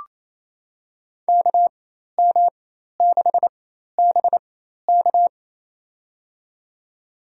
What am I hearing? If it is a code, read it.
Morse code, heard as a keyed tone, about 28 wpm, spaced as for 13 wpm, 700 Hz. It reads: KM6BK
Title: USA Call Signs - 5 Characters - 4x Character Spacing - 28wpm